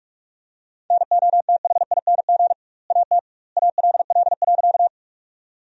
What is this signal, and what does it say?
Morse code, heard as a keyed tone, about 42 wpm, 700 Hz. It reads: NOTHING AT ALL.
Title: 2x Word-Spacing - Sentences from Top 500 Words - 45wpm